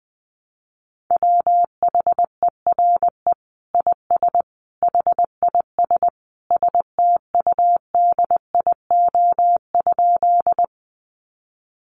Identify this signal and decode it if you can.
Morse code, heard as a keyed tone, about 20 wpm, 700 Hz. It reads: WHERE IS HIS STUDIO?
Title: Sentences from Top 1200 Words 20wpm